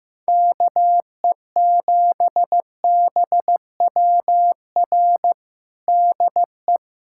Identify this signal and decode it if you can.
Morse code, heard as a keyed tone, about 15 wpm, 700 Hz. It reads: KE7BWR DE
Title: USA Callsign DE Callsign - 6 Characters - 15wpm